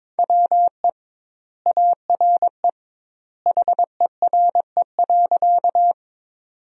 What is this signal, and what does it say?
Morse code, heard as a keyed tone, about 22 wpm, 700 Hz. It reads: WE ARE HERE.
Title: 2x Word-Spacing - Sentences from Top 200 Words - 22wpm